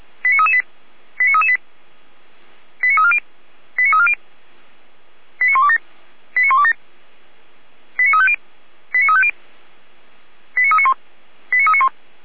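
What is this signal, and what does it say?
Five-tone selective call (ZVEI), heard as quick runs of five tones, tones of about 80 ms. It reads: 89338 89338 89340 89340 89127 89127 89350 89350 89391 89391
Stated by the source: ZVEI five-tone selective call received with an SDR.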